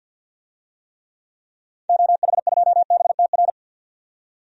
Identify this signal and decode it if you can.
Morse code, heard as a keyed tone, about 49 wpm, 700 Hz. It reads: OH2BTR
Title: DXCC-100 Call Signs - 6 Characters - 50wpm